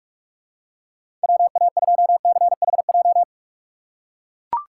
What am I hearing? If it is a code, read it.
Morse code, heard as a keyed tone, about 41 wpm, 700 Hz. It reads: WA2CHJ
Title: USA Call Signs - 4x Character Spacing - 45wpm